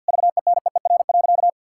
Morse code, heard as a keyed tone, about 51 wpm, 700 Hz. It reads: VENEER.
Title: Pangram Challenge v2 - 50wpm